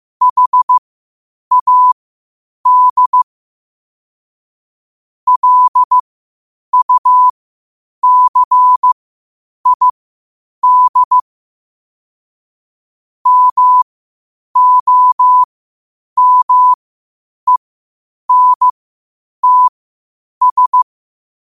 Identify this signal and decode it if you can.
Morse code, heard as a keyed tone, about 15 wpm, spaced as for 8 wpm, 1000 Hz. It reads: HAD LUCID MOMENTS
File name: qotdfiles / 8wpm